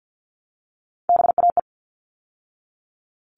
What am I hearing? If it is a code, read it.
Morse code, heard as a keyed tone, about 50 wpm, 700 Hz. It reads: BAE
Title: All Cut Numbers - 3-Digit 50wpm